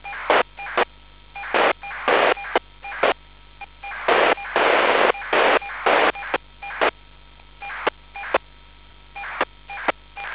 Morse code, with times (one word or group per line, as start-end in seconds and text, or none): none